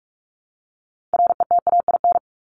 Morse code, heard as a keyed tone, about 44 wpm, 700 Hz. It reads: RETAIN
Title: Top 1751-2000 Words 45wpm